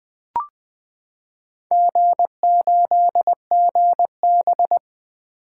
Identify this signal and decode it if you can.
Morse code, heard as a keyed tone, about 20 wpm, 700 Hz. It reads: G8GB
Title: Four Letter-Number Combinations - 20wpm